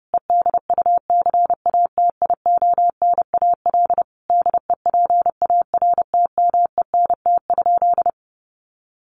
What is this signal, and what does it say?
Morse code, heard as a keyed tone, about 30 wpm, 700 Hz. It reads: EDUCATIONAL DEPARTMENT?
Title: Sentences from Top 1200 Words 30wpm